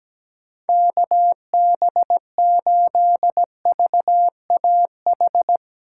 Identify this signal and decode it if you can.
Morse code, heard as a keyed tone, about 17 wpm, 700 Hz. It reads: KB8VAH
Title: USA Call Signs - 6 Characters - 3x Character Spacing - 17wpm